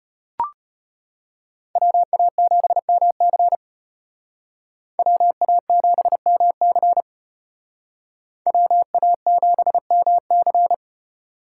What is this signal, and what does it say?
Morse code, heard as a keyed tone, about 32 wpm, 700 Hz. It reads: WA7MC WA7MC WA7MC